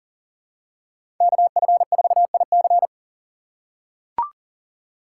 Morse code, heard as a keyed tone, about 40 wpm, 700 Hz. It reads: KF4IC
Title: USA Call Signs - 5 Characters - 4x Character Spacing - 40wpm